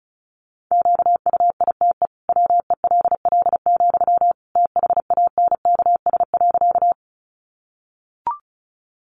Morse code, heard as a keyed tone, about 34 wpm, 700 Hz. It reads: QUITE WELL, THANKS.
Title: Sentences from Top 700 Words 35wpm